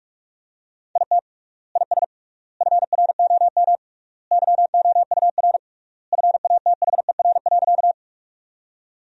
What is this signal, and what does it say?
Morse code, heard as a keyed tone, about 46 wpm, 700 Hz. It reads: IT IS FROM YOUR FATHER.
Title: 3x Word-Spacing - Sentences from Top 300 Words - 45wpm